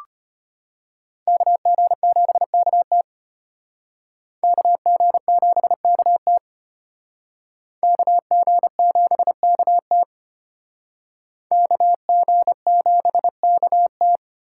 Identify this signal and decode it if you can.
Morse code, heard as a keyed tone, about 33 wpm, 700 Hz. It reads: KG7KT KG7KT KG7KT KG7KT